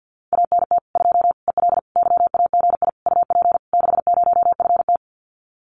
Morse code, heard as a keyed tone, about 50 wpm, 700 Hz. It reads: ANT 2 EL YAGI UP 60FT
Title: Common QSO Phrases Encore 50wpm